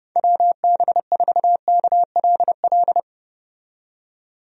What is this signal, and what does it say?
Morse code, heard as a keyed tone, about 32 wpm, 700 Hz. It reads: WB4KLL
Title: USA Call Signs - 6 Characters - 30wpm